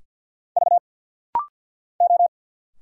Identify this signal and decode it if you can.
Morse code, heard as a keyed tone, about 50 wpm, 700 Hz. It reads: V O
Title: Single Letters - Rapid-Fire 50wpm